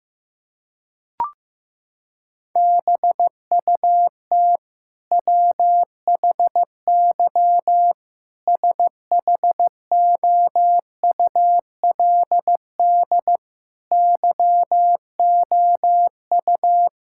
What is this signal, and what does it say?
Morse code, heard as a keyed tone, about 15 wpm, 700 Hz. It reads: BUT WHY SHOULD YOU
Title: Sentences from Top 1000 Words - Review Part 6 - 15wpm